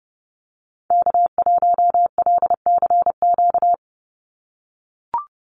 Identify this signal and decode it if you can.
Morse code, heard as a keyed tone, about 30 wpm, 700 Hz. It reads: K1LCQ
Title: USA Call Signs 30wpm